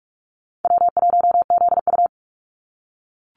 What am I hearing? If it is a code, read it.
Morse code, heard as a keyed tone, about 44 wpm, 700 Hz. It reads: W1ZU
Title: NA SOTA Activator Callsigns 45wpm